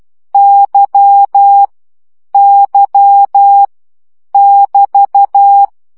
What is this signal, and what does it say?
Morse code, heard as a keyed tone, about 12 wpm, 800 Hz. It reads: Y Y <BT>
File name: CW gyakorlas y 10 baud